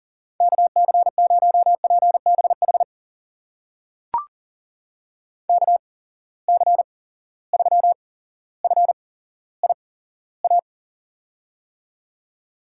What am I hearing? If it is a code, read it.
Morse code, heard as a keyed tone, about 40 wpm, spaced as for 17 wpm, 700 Hz. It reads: KC0PBH KC3FIA
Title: USA Call Signs - 6 Characters - 8x Character Spacing - 40wpm